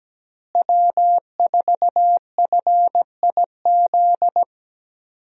Morse code, heard as a keyed tone, about 17 wpm, 700 Hz. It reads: W4FIZ
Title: USA Call Signs - 5 Characters - Repeated 2x - 17wpm